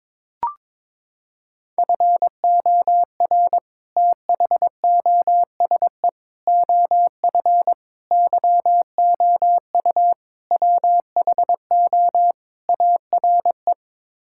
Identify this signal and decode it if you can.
Morse code, heard as a keyed tone, about 22 wpm, 700 Hz. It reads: FOR THOSE OF YOU WHO ARE
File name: practice / 22wpm / Sets of 6 Words - 22wpm